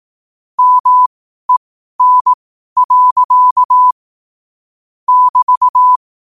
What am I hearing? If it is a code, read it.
Morse code, heard as a keyed tone, about 18 wpm, spaced as for 12 wpm, 1000 Hz. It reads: MEN. <BT>
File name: qotdfiles / 12wpm